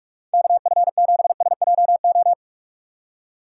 Morse code, heard as a keyed tone, about 49 wpm, 700 Hz. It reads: KF7SJO